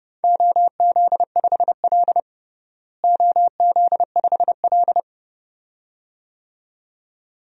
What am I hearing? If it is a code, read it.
Morse code, heard as a keyed tone, about 32 wpm, 700 Hz. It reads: OZ5L OZ5L
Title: DXCC-100 Call Signs - 4 Characters - Repeated 2x - 30wpm